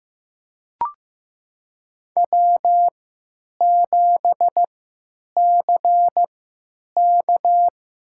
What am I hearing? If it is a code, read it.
Morse code, heard as a keyed tone, about 15 wpm, spaced as for 8 wpm, 700 Hz. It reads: W7CK
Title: USA Call Signs - 3x Character Spacing - 15wpm